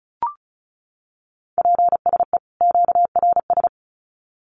Morse code, heard as a keyed tone, about 34 wpm, 700 Hz. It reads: PSE QRS
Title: Common QSO Phrases 35wpm